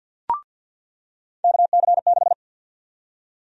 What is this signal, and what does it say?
Morse code, heard as a keyed tone, about 45 wpm, 700 Hz. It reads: KC6
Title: USA Call Sign Prefixes - 50wpm